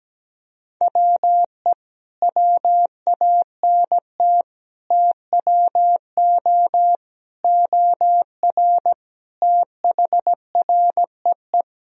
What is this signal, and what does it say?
Morse code, heard as a keyed tone, about 17 wpm, 700 Hz. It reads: WE WANT TWO OR THREE
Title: Sentences from Top 1000 Words - Review Part 1 - 17wpm